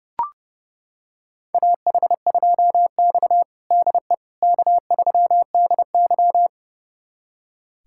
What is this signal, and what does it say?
Morse code, heard as a keyed tone, about 30 wpm, 700 Hz. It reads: AH2X DE K3DY
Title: USA Callsign DE Callsign - 4 Characters - 30wpm